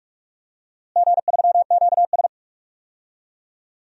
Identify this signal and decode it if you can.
Morse code, heard as a keyed tone, about 46 wpm, 700 Hz. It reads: G3QS
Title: DXCC-100 Call Signs - 4 Characters - Repeated 2x - 45wpm